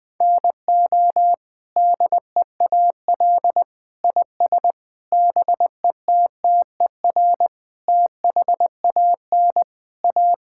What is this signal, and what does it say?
Morse code, heard as a keyed tone, about 20 wpm, 700 Hz. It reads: NO DEAL IS BETTER THAN A
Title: Sentences with 8-Word Phrases - 20wpm